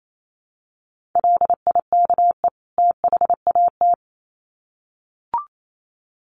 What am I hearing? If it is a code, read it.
Morse code, heard as a keyed tone, about 28 wpm, 700 Hz. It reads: LIKE THAT
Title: Sets of 2 Words 28wpm